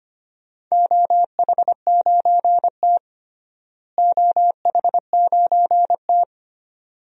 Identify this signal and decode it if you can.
Morse code, heard as a keyed tone, about 25 wpm, 700 Hz. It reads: OH9T OH9T
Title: DXCC-100 Call Signs - 4 Characters - Repeated 2x - 25wpm